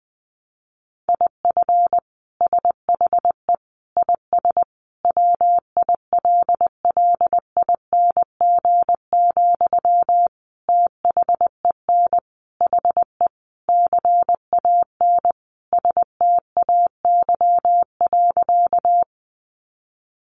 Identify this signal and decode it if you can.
Morse code, heard as a keyed tone, about 20 wpm, 700 Hz. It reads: IF SHE IS WILLING, THEN HE CAN STAY.
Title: Sentences from Top 1000 Words 20wpm